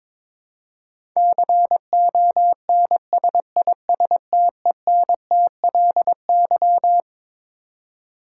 Morse code, heard as a keyed tone, about 22 wpm, 700 Hz. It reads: CONSISTENTLY
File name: Twelve Letter Words - 22wpm